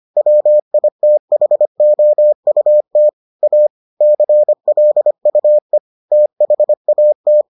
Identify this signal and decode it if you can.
Morse code, heard as a keyed tone, about 25 wpm, 600 Hz. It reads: WITHOUT A CLUE THAT